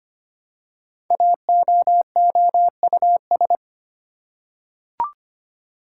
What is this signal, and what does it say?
Morse code, heard as a keyed tone, about 25 wpm, 700 Hz. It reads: AOOUS
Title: Code Groups - 5 Letters - 25wpm